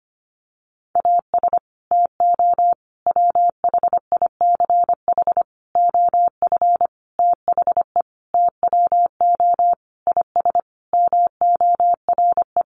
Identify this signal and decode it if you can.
Morse code, heard as a keyed tone, about 25 wpm, 700 Hz. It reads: AS TO WHICH OF THE TWO IS MORE
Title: Sentences from Top 1200 Words 25wpm